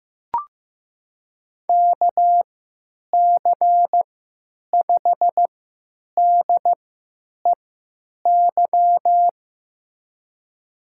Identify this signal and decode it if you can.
Morse code, heard as a keyed tone, about 16 wpm, spaced as for 9 wpm, 700 Hz. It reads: KC5DEY
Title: USA Call Signs - 6 Characters - 3x Character Spacing - 15wpm